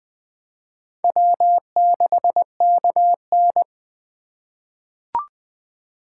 Morse code, heard as a keyed tone, about 20 wpm, 700 Hz. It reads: W6KN
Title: USA Call Signs - 4 Characters - 8x Character Spacing - 20wpm